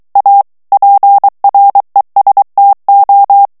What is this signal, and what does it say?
Morse code, heard as a keyed tone, about 23 wpm, 800 Hz. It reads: A PRESTO
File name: morse mp3 / ITALIEN texte clair 1 vitesse 22 mots minute